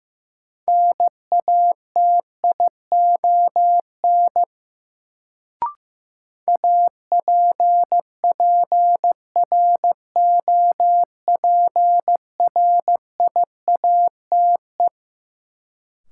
Morse code, heard as a keyed tone, about 15 wpm, 700 Hz. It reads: NATION APPROPRIATE
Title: Sentences from Top 900 Words - New Word Drill - 15wpm